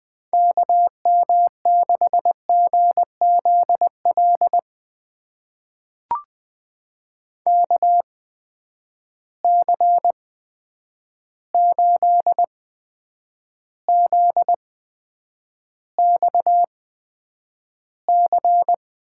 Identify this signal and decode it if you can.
Morse code, heard as a keyed tone, about 20 wpm, spaced as for 9 wpm, 700 Hz. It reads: KM6GZL KC8ZXC
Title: USA Call Signs - 6 Characters - 8x Character Spacing - 20wpm